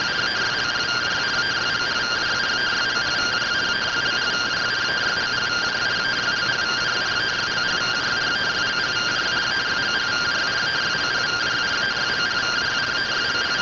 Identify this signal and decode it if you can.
Two-tone teleprinter signal, about 170 Hz shift.